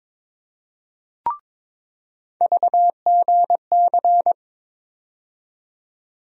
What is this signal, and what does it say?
Morse code, heard as a keyed tone, about 22 wpm, 700 Hz. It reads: VGC
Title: USA Call Sign Postfixes - 3 Letters 22wpm